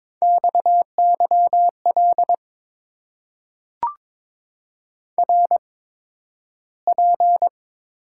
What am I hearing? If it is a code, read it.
Morse code, heard as a keyed tone, about 22 wpm, spaced as for 12 wpm, 700 Hz. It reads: XYL RP
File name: Top 90 QSO Elements - 8x Character Spacing - 22wpm